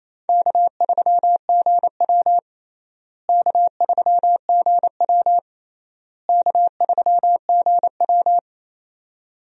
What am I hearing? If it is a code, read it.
Morse code, heard as a keyed tone, about 27 wpm, 700 Hz. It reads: K3GW K3GW K3GW
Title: USA Call Signs - 4 Characters - Repeated 3x - 28wpm